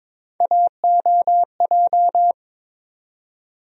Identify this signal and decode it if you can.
Morse code, heard as a keyed tone, about 22 wpm, 700 Hz. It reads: AOJ
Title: USA Call Sign Postfixes - 3 Letters 22wpm